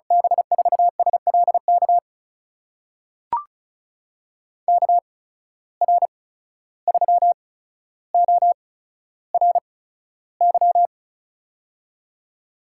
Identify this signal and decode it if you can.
Morse code, heard as a keyed tone, about 36 wpm, spaced as for 14 wpm, 700 Hz. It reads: B4SLK KR3ORY
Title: USA Call Signs - 6 Characters - 8x Character Spacing - 35wpm